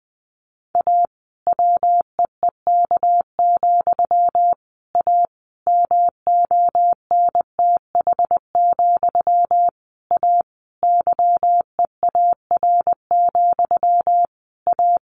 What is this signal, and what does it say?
Morse code, heard as a keyed tone, about 19 wpm, 700 Hz. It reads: A WEEK, A MONTH, A YEAR, A
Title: A Sound Of Thunder 20wpm